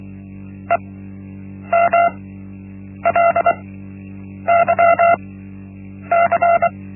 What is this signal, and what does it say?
Morse code, heard as a keyed tone, about 23 wpm, spaced as for 8 wpm, 680 Hz. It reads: EMLYC
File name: Linksmakalnis-1